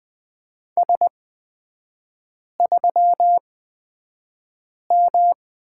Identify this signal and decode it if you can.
Morse code, heard as a keyed tone, about 20 wpm, spaced as for 5 wpm, 700 Hz. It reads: S3M
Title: 255 Lesson 41e - Period - Three Characters 20wpm